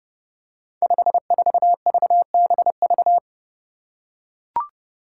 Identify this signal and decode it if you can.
Morse code, heard as a keyed tone, about 28 wpm, 700 Hz. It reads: 54VBV
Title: Five Challenging Letter-Number Combinations 30wpm